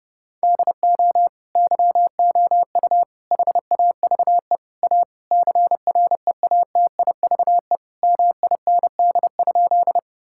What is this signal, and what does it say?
Morse code, heard as a keyed tone, about 32 wpm, 700 Hz. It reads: DO YOU HAVE A CREATIVE MIND?
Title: Sentences from Top 1000 Words - Review Part 5 - 30wpm